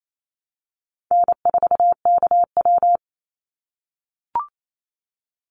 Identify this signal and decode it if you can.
Morse code, heard as a keyed tone, about 28 wpm, 700 Hz. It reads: N4KW
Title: Easy Call Signs - 28wpm